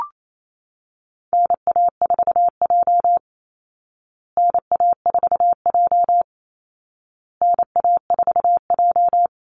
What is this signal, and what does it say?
Morse code, heard as a keyed tone, about 28 wpm, 700 Hz. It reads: NA4J NA4J NA4J
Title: Call Signs Repeated 2x - 28wpm